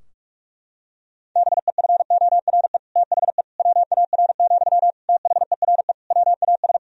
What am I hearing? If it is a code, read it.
Morse code, heard as a keyed tone, about 46 wpm, 700 Hz. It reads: BEFORE THE WAR, THERE WAS
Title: Sentences from Top 1000 Words - Review Part 2 - 45wpm